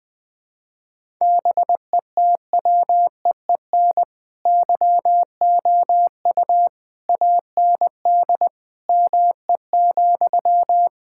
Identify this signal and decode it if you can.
Morse code, heard as a keyed tone, about 21 wpm, 700 Hz. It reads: BETWEEN YOU AND ME,
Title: All-Age Appropriate Jokes - 20wpm